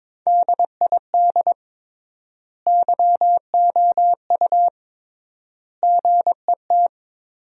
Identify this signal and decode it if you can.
Morse code, heard as a keyed tone, about 21 wpm, 700 Hz. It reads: DID YOU GET
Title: 3x Word-Spacing - Sentences from Top 400 Words - 22wpm